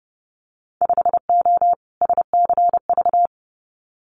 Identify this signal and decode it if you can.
Morse code, heard as a keyed tone, about 30 wpm, 700 Hz. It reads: 5O SCV
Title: Winter Field Day Exchanges Encore 30wpm